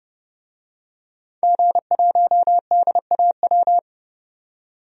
Morse code, heard as a keyed tone, about 30 wpm, 700 Hz. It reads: G1DAW
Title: DXCC-100 Call Signs - 5 Characters - 30wpm